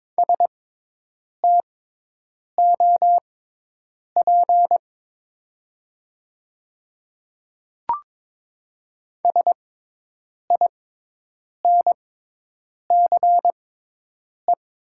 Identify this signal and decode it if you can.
Morse code, heard as a keyed tone, about 22 wpm, spaced as for 8 wpm, 700 Hz. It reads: STOP SINCE
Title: 6x Character Spacing - Top 300 Words - 22wpm